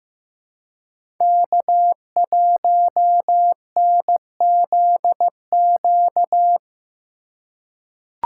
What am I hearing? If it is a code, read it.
Morse code, heard as a keyed tone, about 15 wpm, 700 Hz. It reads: K1NZQ
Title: USA Call Signs - 5 Characters - 4x Character Spacing - 15wpm